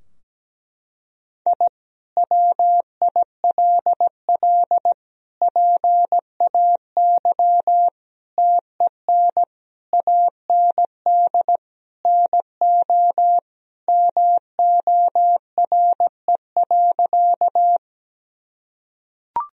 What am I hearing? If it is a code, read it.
Morse code, heard as a keyed tone, about 17 wpm, 700 Hz. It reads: I WILL PAY TEN AND NO MORE.
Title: Sentences from Top 1000 Words - Review Part 2 - 17wpm